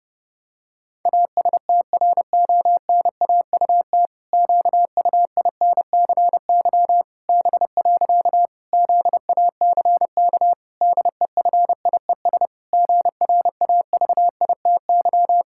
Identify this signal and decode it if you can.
Morse code, heard as a keyed tone, about 30 wpm, 700 Hz. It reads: ASTRONAUT QUINCY B. ZACK DEFIES GRAVITY
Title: Pangram Challenge v2 - 30wpm